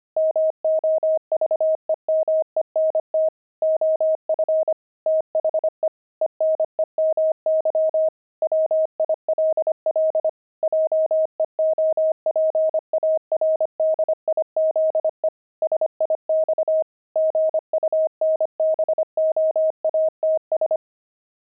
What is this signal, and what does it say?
Morse code, heard as a keyed tone, about 25 wpm, 630 Hz. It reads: MOVEMENT OF THE ENEMY WILL JEOPARDIZE SIX GUNBOATS
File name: pangram-fs-25-cs-25-enemy